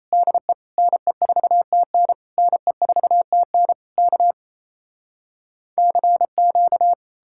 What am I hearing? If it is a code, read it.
Morse code, heard as a keyed tone, about 37 wpm, 700 Hz. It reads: DE NE4TN NE4TN K CQ